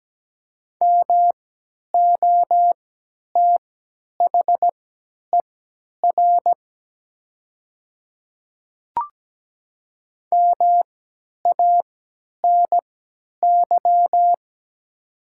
Morse code, heard as a keyed tone, about 17 wpm, spaced as for 10 wpm, 700 Hz. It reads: MOTHER MANY
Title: 3x Character Spacing - Top 200 Words - 17wpm